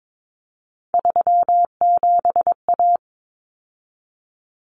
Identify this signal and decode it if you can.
Morse code, heard as a keyed tone, about 22 wpm, 700 Hz. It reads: 37A
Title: Common Cut Numbers - 3-Digit 22wpm